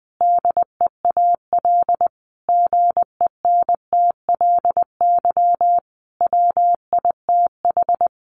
Morse code, heard as a keyed tone, about 20 wpm, 700 Hz. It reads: DEAL GENTLY WITH
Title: Sentences from Top 1500 Words 20wpm